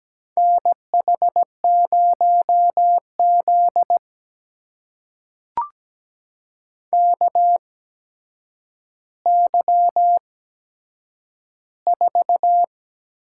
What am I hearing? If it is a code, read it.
Morse code, heard as a keyed tone, about 17 wpm, spaced as for 8 wpm, 700 Hz. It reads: NH0Z KY4
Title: USA Call Signs - 4 Characters - 8x Character Spacing - 17wpm